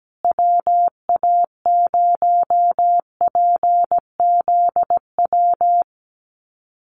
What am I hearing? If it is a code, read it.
Morse code, heard as a keyed tone, about 17 wpm, 700 Hz. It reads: WA0PZW
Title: USA Call Signs Repeated 3x 17wpm